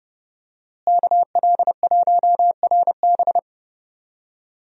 Morse code, heard as a keyed tone, about 30 wpm, 700 Hz. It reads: KL1RB
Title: USA Call Signs - 5 Characters - Repeated 3x - 30wpm